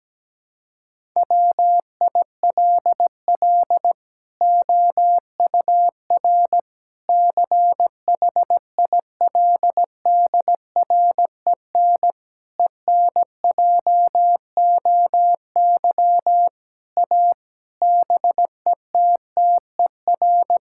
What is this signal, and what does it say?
Morse code, heard as a keyed tone, about 17 wpm, 700 Hz. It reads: WILL OUR CHILDREN ENJOY A BETTER
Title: Sentences from Top 1000 Words - Review Part 5 - 17wpm